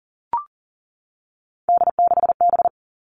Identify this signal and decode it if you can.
Morse code, heard as a keyed tone, about 41 wpm, 700 Hz. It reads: D6B
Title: Three Challenging Letter-Number Combinations 40wpm